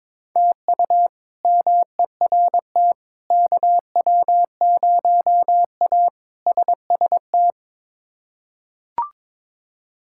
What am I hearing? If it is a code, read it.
Morse code, heard as a keyed tone, about 22 wpm, 700 Hz. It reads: TU MERT KW0A SST
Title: K1USN SST QSO Practice - 22wpm